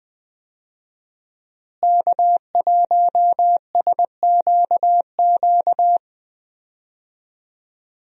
Morse code, heard as a keyed tone, about 20 wpm, 700 Hz. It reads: K1SQQ
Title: USA Call Signs - 5 Characters - Repeated 3x - 20wpm